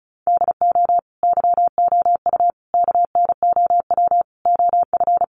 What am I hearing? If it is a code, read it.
Morse code, heard as a keyed tone, about 34 wpm, 700 Hz. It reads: DO YOU KNOW OF
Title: Sentences from Top 1200 Words 35wpm